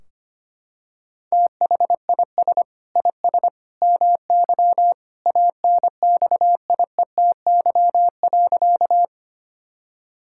Morse code, heard as a keyed tone, about 25 wpm, 700 Hz. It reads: THIS IS MY ANXIETY.
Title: Sentences from Top 1600 Words 25wpm